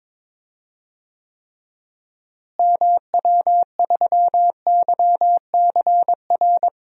Morse code, heard as a keyed tone, about 22 wpm, 700 Hz. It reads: MW3YCR
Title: DXCC-100 Call Signs - 6 Characters - Repeated 3x - 22wpm